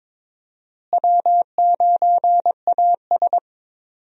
Morse code, heard as a keyed tone, about 22 wpm, 700 Hz. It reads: W9AS
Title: USA Call Signs - 4 Characters - Repeated 3x - 22wpm